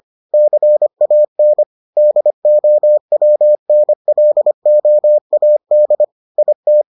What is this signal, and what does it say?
Morse code, heard as a keyed tone, about 25 wpm, 600 Hz. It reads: CAN DOWNLOAD IT